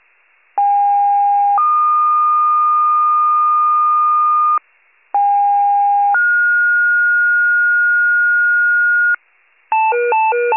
Two-tone page, heard as a one-second tone, then a three-second tone, about 810 then 1220 Hz, about 810 then 1430 Hz, with noise. It is heard from 0.6 to 9.1 s.